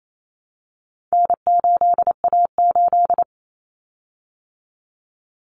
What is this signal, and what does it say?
Morse code, heard as a keyed tone, about 28 wpm, 700 Hz. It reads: N8A8
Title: Common Cut Numbers - 4-Digit 28wpm